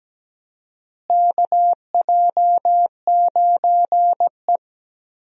Morse code, heard as a keyed tone, about 17 wpm, 700 Hz. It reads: KJ9E
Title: USA Call Signs - 4 Characters - Repeated 3x - 17wpm